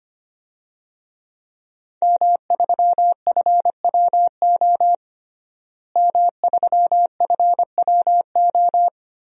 Morse code, heard as a keyed tone, about 25 wpm, 700 Hz. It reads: M3FWO M3FWO
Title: DXCC-100 Call Signs - 5 Characters - Repeated 3x - 25wpm